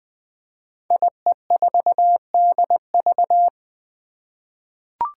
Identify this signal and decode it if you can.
Morse code, heard as a keyed tone, about 20 wpm, 700 Hz. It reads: IE4DV
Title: Five Challenging Letter-Number Combinations 20wpm